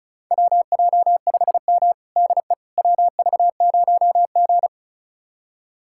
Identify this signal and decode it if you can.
Morse code, heard as a keyed tone, about 36 wpm, 700 Hz. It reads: WJ5M DE WV0G
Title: USA Callsign DE Callsign - 4 Characters - 35wpm